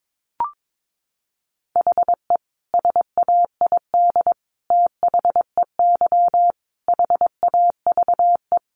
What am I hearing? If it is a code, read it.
Morse code, heard as a keyed tone, about 22 wpm, 700 Hz. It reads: HE SAID THEY HAVE
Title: Sentences from Top 800 Words 22wpm